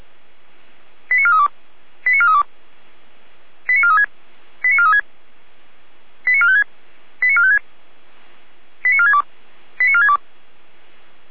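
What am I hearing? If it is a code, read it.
Five-tone selective call (ZVEI), heard as quick runs of five tones, tones of about 80 ms. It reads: 89432 89432 89436 89436 89456 89456 89462 89462
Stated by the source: ZVEI five-tone selective call 89380, received on an SDR.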